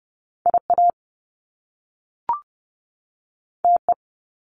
Morse code, heard as a keyed tone, about 30 wpm, 700 Hz. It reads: I A T E
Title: Top Two Letter Combinations in Words 30wpm